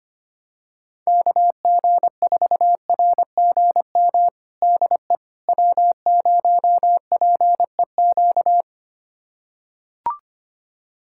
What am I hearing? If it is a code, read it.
Morse code, heard as a keyed tone, about 25 wpm, 700 Hz. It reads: KG4RGM DE W0PEQ
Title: USA Callsign DE Callsign 25wpm